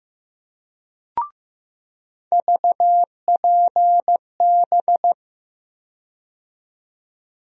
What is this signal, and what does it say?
Morse code, heard as a keyed tone, about 15 wpm, 700 Hz. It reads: VPB
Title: USA Call Sign Postfixes - 3 Letters 15wpm